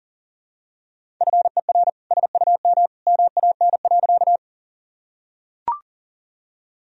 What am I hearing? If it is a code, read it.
Morse code, heard as a keyed tone, about 40 wpm, 700 Hz. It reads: FER SUM MAN.
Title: Sentences from Top 200 Words and Abbreviations 40wpm